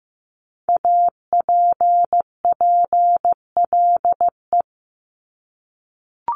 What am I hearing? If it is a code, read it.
Morse code, heard as a keyed tone, about 15 wpm, 700 Hz. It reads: APPLE
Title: Multi-letter Phonemes Repeated 3x 15wpm